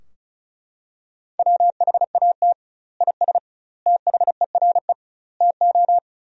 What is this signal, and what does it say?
Morse code, heard as a keyed tone, about 34 wpm, 700 Hz. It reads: WHAT IS THERE TO
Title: 2x Word-Spacing - Sentences from Top 100 Words - 35wpm